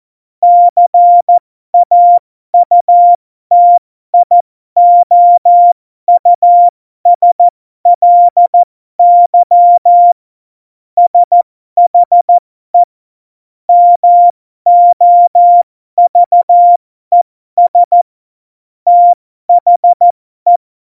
Morse code, heard as a keyed tone, about 14 wpm, spaced as for 12 wpm, 700 Hz. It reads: CAUTIOUSLY SHE MOVES THE
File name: Morsekurs / Witze